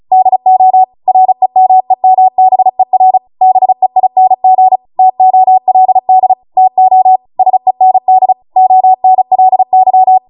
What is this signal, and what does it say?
Morse code, heard as a keyed tone, about 36 wpm, 750 Hz. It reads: DO REMEMBER BEING TOLD TO SEND ONLY